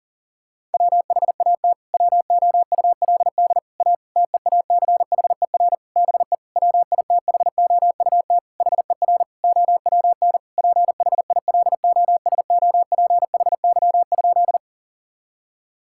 Morse code, heard as a keyed tone, about 40 wpm, 700 Hz. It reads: WHAT WOULD A TEACHER BE WITHOUT HER OWN PHILOSOPHY?
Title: Sentences from Top 1000 Words - Review Part 6 - 40wpm